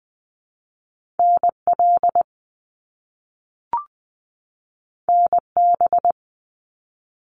Morse code, heard as a keyed tone, about 20 wpm, 700 Hz. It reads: NL NB
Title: Canadian Provinces and Territories 20wpm